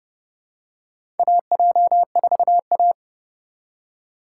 Morse code, heard as a keyed tone, about 30 wpm, 700 Hz. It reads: AJ4A
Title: USA Call Signs - 4 Characters - 2x Character Spacing - 30wpm